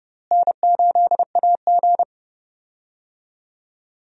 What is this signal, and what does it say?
Morse code, heard as a keyed tone, about 30 wpm, 700 Hz. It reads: N8AG
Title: USA Call Signs - 4 Characters - Repeated 2x - 30wpm